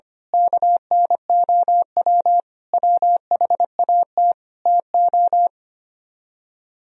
Morse code, heard as a keyed tone, about 25 wpm, 700 Hz. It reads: KNOW WHAT TO
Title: Speed-Racing - Sets of 5 Words 25wpm